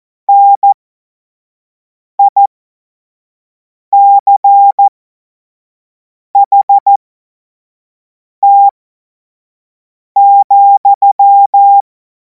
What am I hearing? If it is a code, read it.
Morse code, heard as a keyed tone, about 14 wpm, spaced as for 5 wpm, 800 Hz. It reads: NICHT,
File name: Morsekurs / Witze / Witz 12. Mai 2026 25 F70